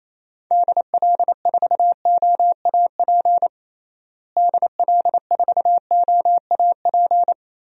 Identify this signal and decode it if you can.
Morse code, heard as a keyed tone, about 27 wpm, 700 Hz. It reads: DL4OAP DL4OAP
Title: USA Call Signs - 6 Characters - Repeated 3x - 28wpm